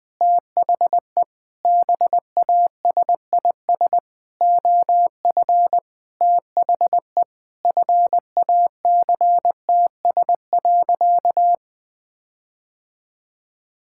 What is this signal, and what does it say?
Morse code, heard as a keyed tone, about 20 wpm, 700 Hz. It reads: THE BASIS OF THE FACTS.
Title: Sentences with 5-Word Phrases - Spoken Beginning - 20wpm